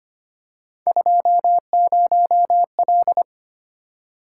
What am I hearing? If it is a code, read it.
Morse code, heard as a keyed tone, about 25 wpm, 700 Hz. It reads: 20L
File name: Three Letter-Number Combinations - 25wpm